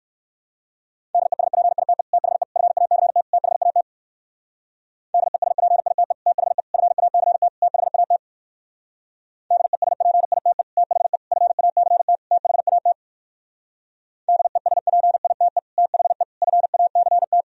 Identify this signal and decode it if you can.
Morse code, heard as a keyed tone, about 58 wpm, 700 Hz. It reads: DESPITE THE FACT THAT DESPITE THE FACT THAT DESPITE THE FACT THAT DESPITE THE FACT